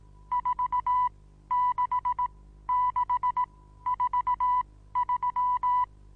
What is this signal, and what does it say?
Morse code, heard as a keyed tone, about 18 wpm, spaced as for 13 wpm, 990 Hz. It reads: 46643
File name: Onlinelehrgang / morsen / DJ4UF Morsen25